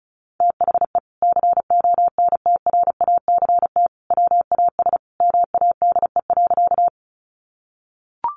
Morse code, heard as a keyed tone, about 34 wpm, 700 Hz. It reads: THE CONTRACT WAS MADE.
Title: Sentences from Top 1400 Words 35wpm